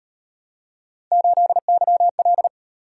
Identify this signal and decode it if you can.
Morse code, heard as a keyed tone, about 37 wpm, 700 Hz. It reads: 8YL